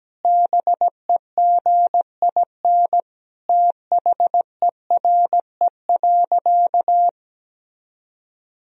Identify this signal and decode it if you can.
Morse code, heard as a keyed tone, about 17 wpm, 700 Hz. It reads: BEGIN THERE.
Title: Sentences from Top 1000 Words - Review Part 1 - 17wpm